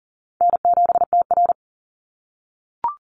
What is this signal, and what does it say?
Morse code, heard as a keyed tone, about 40 wpm, 700 Hz. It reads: N7TR
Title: Easy Call Signs - 40wpm